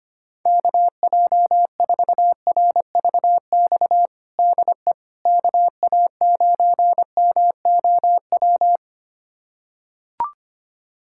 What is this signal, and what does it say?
Morse code, heard as a keyed tone, about 25 wpm, 700 Hz. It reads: KJ4RVX DE KA9MOW
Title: USA Callsign DE Callsign - 6 Characters - 25wpm